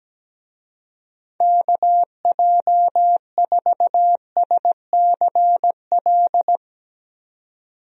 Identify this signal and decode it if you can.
Morse code, heard as a keyed tone, about 17 wpm, 700 Hz. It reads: KJ4SCL
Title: USA Call Signs - 6 Characters - 17wpm